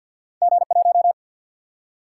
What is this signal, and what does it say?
Morse code, heard as a keyed tone, about 53 wpm, 700 Hz. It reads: G 1
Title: Single-Letter Single-Number 50wpm